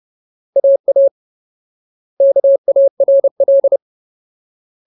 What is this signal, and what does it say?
Morse code, heard as a keyed tone, about 30 wpm, 550 Hz. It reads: AA KARL
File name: Ultimate Word List / 30WPM Ultimate